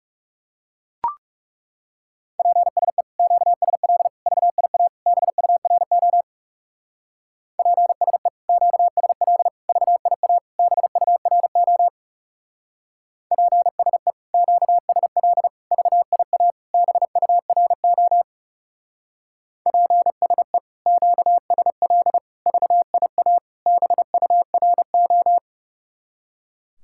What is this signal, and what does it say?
Morse code, heard as a keyed tone, about 35 wpm, 700 Hz. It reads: PSE QSL VIA BURO PSE QSL VIA BURO PSE QSL VIA BURO PSE QSL VIA BURO
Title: Speed-Racing Common QSO Phrases Encore - 30wpm